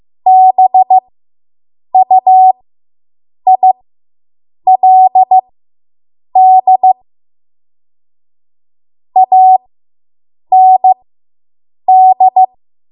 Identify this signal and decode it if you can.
Morse code, heard as a keyed tone, about 15 wpm, spaced as for 7 wpm, 750 Hz. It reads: BUILD AND